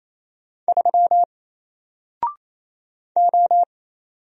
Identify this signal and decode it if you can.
Morse code, heard as a keyed tone, about 27 wpm, 700 Hz. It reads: 3 O
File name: Single-Letter-Number-28wpm